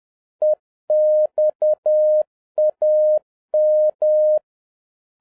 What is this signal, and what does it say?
Morse code, heard as a keyed tone, about 10 wpm, 610 Hz. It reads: EXAM